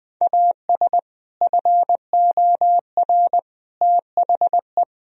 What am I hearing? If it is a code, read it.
Morse code, heard as a keyed tone, about 20 wpm, 700 Hz. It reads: AS FOR THE
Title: Sentences from Top 1000 Words - Review Part 1 - 20wpm